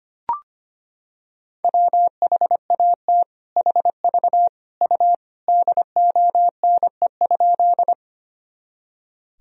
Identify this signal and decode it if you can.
Morse code, heard as a keyed tone, about 25 wpm, 700 Hz. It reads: WHAT HV U DONE?
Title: Sentences from Top 500 Words and Abbreviations 25wpm